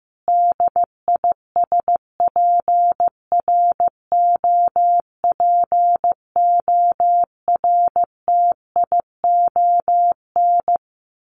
Morse code, heard as a keyed tone, about 15 wpm, 700 Hz. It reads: DISPROPORTION
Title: A Sound Of Thunder 15wpm